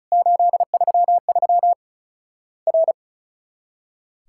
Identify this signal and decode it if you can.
Morse code, heard as a keyed tone, about 34 wpm, 700 Hz. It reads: 833 R
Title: Worked All Europe DX Contest - QTC Exchange - 35wpm